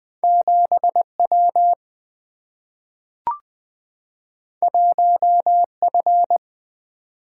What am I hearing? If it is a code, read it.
Morse code, heard as a keyed tone, about 20 wpm, 700 Hz. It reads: 7 W 1 F
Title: Single-Letter Single-Number 20wpm